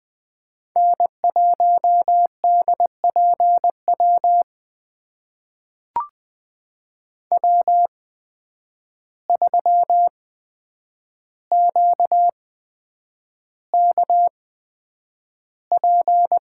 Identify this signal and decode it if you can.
Morse code, heard as a keyed tone, about 20 wpm, spaced as for 9 wpm, 700 Hz. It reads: N1DPW W3QKP
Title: USA Call Signs - 5 Characters - 8x Character Spacing - 20wpm